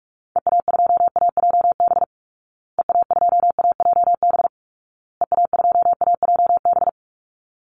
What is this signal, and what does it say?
Morse code, heard as a keyed tone, about 44 wpm, 700 Hz. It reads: EA2AJB EA2AJB EA2AJB
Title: Call Signs Repeated 2x - 45wpm